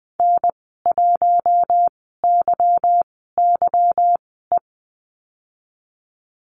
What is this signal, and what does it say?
Morse code, heard as a keyed tone, about 20 wpm, spaced as for 15 wpm, 700 Hz. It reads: N1YYE
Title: 173 Lesson 28g - Number 9 - Call Signs 20wpm